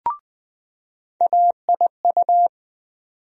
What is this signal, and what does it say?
Morse code, heard as a keyed tone, about 20 wpm, 700 Hz. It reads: AIU
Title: USA Call Sign Postfixes - 3 Letters 20wpm